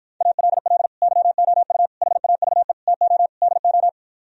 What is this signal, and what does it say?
Morse code, heard as a keyed tone, about 58 wpm, 700 Hz. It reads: ALL YOU HAVE TO DO